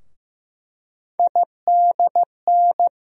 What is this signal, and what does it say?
Morse code, heard as a keyed tone, about 15 wpm, 700 Hz. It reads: IDN
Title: USA Call Sign Postfixes - 3 Letters 15wpm